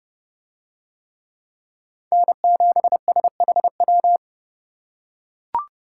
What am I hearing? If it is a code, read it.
Morse code, heard as a keyed tone, about 28 wpm, 700 Hz. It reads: N7SHW